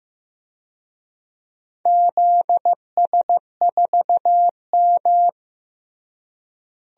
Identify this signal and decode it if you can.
Morse code, heard as a keyed tone, about 15 wpm, 700 Hz. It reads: ZS4M
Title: DXCC-100 Call Signs - 4 Characters - Repeated 2x - 15wpm